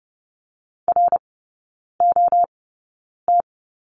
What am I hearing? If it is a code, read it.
Morse code, heard as a keyed tone, about 30 wpm, spaced as for 9 wpm, 700 Hz. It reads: ROT
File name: 081 Lesson 14e - Number 2 - Three Characters 30wpm